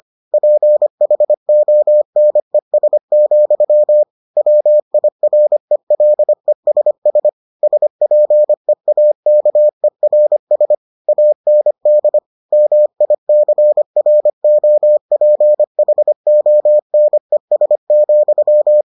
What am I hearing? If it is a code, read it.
Morse code, heard as a keyed tone, about 25 wpm, 600 Hz. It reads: PHONES, WIRELESS SPEAKERS AND MICROPHONES,